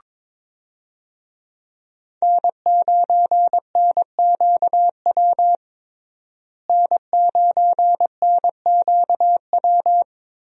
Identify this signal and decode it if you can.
Morse code, heard as a keyed tone, about 22 wpm, 700 Hz. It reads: N9NQW N9NQW
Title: USA Call Signs - 5 Characters - Repeated 3x - 22wpm